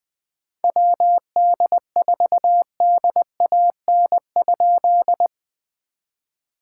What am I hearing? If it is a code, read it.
Morse code, heard as a keyed tone, about 20 wpm, 700 Hz. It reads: WD4DAN?
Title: POTA QSO Practice - 20wpm